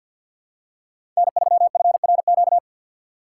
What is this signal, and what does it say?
Morse code, heard as a keyed tone, about 53 wpm, 700 Hz. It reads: N3FRQ
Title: USA Call Signs - 5 Characters - 4x Character Spacing - 50wpm